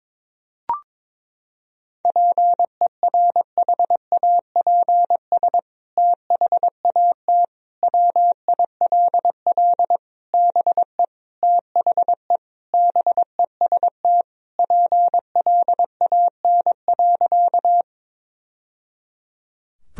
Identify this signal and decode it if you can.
Morse code, heard as a keyed tone, about 22 wpm, 700 Hz. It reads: PERHAPS THAT WILL BE THE BEST PLAN.
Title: Sentences from Top 1000 Words - Review Part 2 - 22wpm